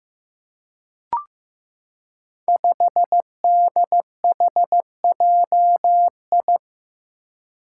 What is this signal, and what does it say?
Morse code, heard as a keyed tone, about 15 wpm, 700 Hz. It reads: 5DHJI
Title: Five Challenging Letter-Number Combinations 15wpm